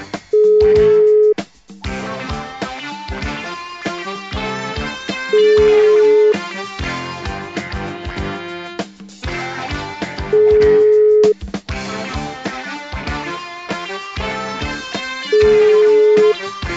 Ringback tone (Europe, 425 Hz).